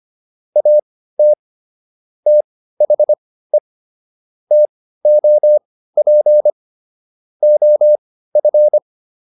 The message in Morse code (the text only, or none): AT THE TOP OF